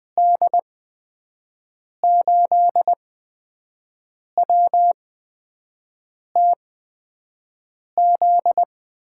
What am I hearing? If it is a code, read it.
Morse code, heard as a keyed tone, about 20 wpm, spaced as for 5 wpm, 700 Hz. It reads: D8WTZ
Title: USA Call Signs - 8x Character Spacing - 20wpm